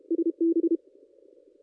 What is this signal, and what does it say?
Morse code, heard as a keyed tone, about 32 wpm, 330 Hz. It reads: SB
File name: contests / ss-cw-k5tr-op-n5rz -1800